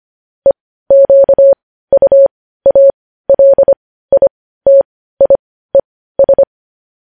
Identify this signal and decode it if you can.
Morse code, heard as a keyed tone, about 25 wpm, spaced as for 15 wpm, 550 Hz. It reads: EQUALITIES
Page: news headlines in morse code at 15 wpm